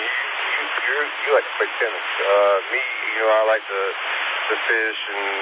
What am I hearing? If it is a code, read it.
Emergency Alert System (SAME) tones (header bursts), heard as harsh buzzing bursts, with speech. It is heard from 0.7 to 5.0 s.